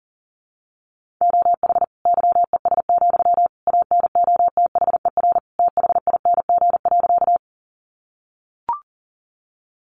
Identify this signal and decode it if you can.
Morse code, heard as a keyed tone, about 40 wpm, 700 Hz. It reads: OH YES, ANOTHER THING.
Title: Sentences from Top 600 Words 40wpm